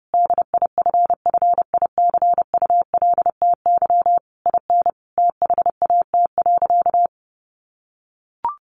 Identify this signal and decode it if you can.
Morse code, heard as a keyed tone, about 30 wpm, 700 Hz. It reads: DIFFICULTY IN THAT.
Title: Sentences from Top 1500 Words 30wpm